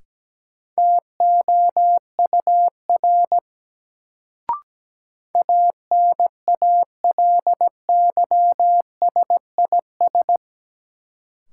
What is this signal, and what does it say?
Morse code, heard as a keyed tone, about 17 wpm, 700 Hz. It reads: TOUR ANALYSIS
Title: Sentences from Top 1400 Words - New Word Drill Alt - 17wpm